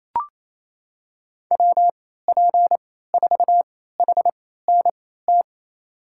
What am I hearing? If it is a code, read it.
Morse code, heard as a keyed tone, about 28 wpm, spaced as for 16 wpm, 700 Hz. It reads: WP4HNT
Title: USA Call Signs - 6 Characters - 3x Character Spacing - 28wpm